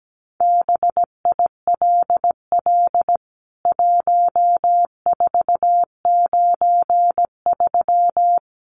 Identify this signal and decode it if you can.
Morse code, heard as a keyed tone, about 17 wpm, 700 Hz. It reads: BILL 1493
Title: CWT Exchange 17wpm